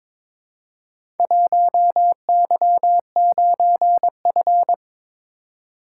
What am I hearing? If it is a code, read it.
Morse code, heard as a keyed tone, about 22 wpm, 700 Hz. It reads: 1Y9F
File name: practice / 22wpm / Four Letter-Number Combinations - 22wpm